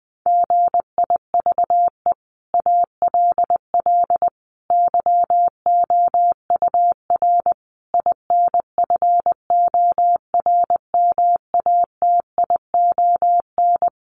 Morse code, heard as a keyed tone, about 20 wpm, 700 Hz. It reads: GIVE ALL YOUR INFORMATION
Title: Sentences from Top 600 Words 20wpm